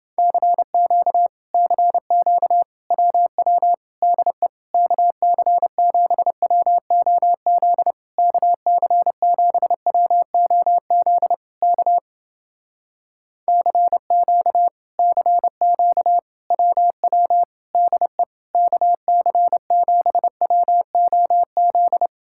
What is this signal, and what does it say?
Morse code, heard as a keyed tone, about 29 wpm, 700 Hz. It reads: CQ CQ WW DE KC7WOZ KC7WOZ K CQ CQ WW DE KC7WOZ